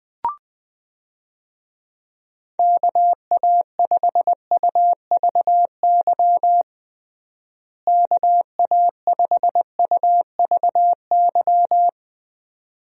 Morse code, heard as a keyed tone, about 20 wpm, 700 Hz. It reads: KA5UVY KA5UVY
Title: USA Call Signs - 6 Characters - Repeated 2x - 20wpm